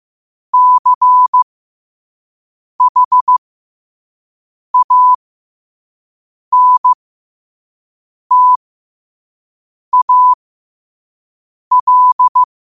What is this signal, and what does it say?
Morse code, heard as a keyed tone, about 15 wpm, spaced as for 5 wpm, 1000 Hz. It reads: CHANTAL
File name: qotdfiles / 5wpm